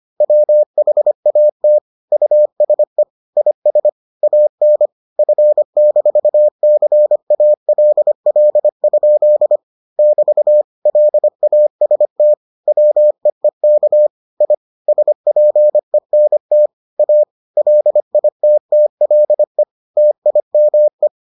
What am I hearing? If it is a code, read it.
Morse code, heard as a keyed tone, about 25 wpm, 600 Hz. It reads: WHAT USE IS AN F-CALL? <BT> LAST WEEK I SPENT A LITTLE TIME